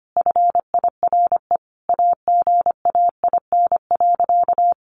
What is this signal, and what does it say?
Morse code, heard as a keyed tone, about 25 wpm, 700 Hz. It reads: FIRE AGAIN.
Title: Sentences from Top 1400 Words 25wpm